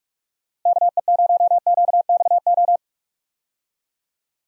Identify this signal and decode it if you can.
Morse code, heard as a keyed tone, about 46 wpm, 700 Hz. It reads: KE0QXO
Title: USA Call Signs - 6 Characters - Repeated 3x - 45wpm